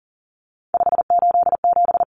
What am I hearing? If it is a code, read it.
Morse code, heard as a keyed tone, about 40 wpm, 700 Hz. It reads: 587
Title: Numbers - 3-Digit 40wpm